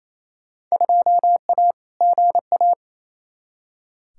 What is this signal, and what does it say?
Morse code, heard as a keyed tone, about 28 wpm, 700 Hz. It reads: 2A GA
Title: Field Day Exchanges Encore - 28wpm